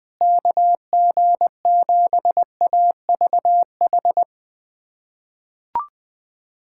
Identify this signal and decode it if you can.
Morse code, heard as a keyed tone, about 20 wpm, 700 Hz. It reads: KG7AVH
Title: USA Callsign DE Callsign 20wpm